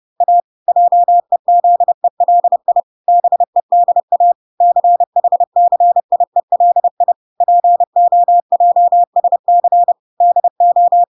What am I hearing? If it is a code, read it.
Morse code, heard as a keyed tone, about 30 wpm, 700 Hz. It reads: A JEZELI BEDA CHCIELI POJSC DO